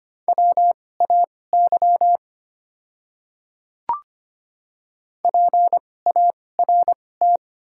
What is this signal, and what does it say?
Morse code, heard as a keyed tone, about 25 wpm, spaced as for 18 wpm, 700 Hz. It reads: WAY PART
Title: 2x Character Spacing - Top 200 Words - 25wpm